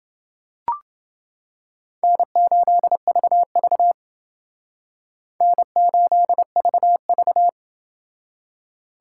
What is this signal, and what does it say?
Morse code, heard as a keyed tone, about 29 wpm, 700 Hz. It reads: N8VV N8VV